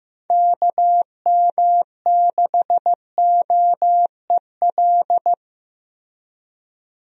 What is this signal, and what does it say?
Morse code, heard as a keyed tone, about 15 wpm, 700 Hz. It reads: KM6OEL
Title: USA Call Signs - 6 Characters - Repeated 3x - 15wpm